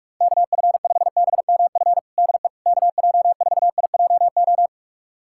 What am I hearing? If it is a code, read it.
Morse code, heard as a keyed tone, about 42 wpm, 700 Hz. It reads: KF5BMF DE KJ4IJO
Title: USA Callsign DE Callsign - 6 Characters - 45wpm